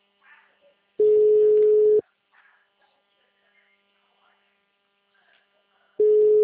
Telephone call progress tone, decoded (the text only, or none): ringback tone